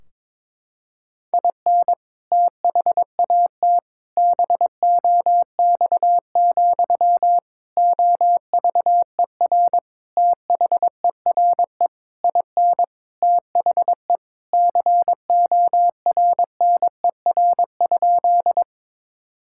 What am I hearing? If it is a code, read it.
Morse code, heard as a keyed tone, about 22 wpm, 700 Hz. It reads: IN THAT BOX, OVER THERE IN THE CORNER?
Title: Sentences from Top 1600 Words 22wpm